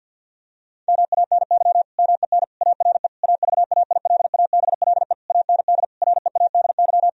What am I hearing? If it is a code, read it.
Morse code, heard as a keyed tone, about 50 wpm, 700 Hz. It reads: MANY MEN ARE AVAILABLE AND READY
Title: Sentences from Top 1000 Words - Review Part 3 - 50wpm